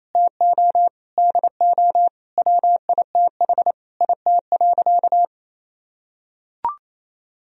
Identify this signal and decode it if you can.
Morse code, heard as a keyed tone, about 27 wpm, 700 Hz. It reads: TO DO WITH IT.
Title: Sentences from Top 1000 Words - Review Part 2 - 28wpm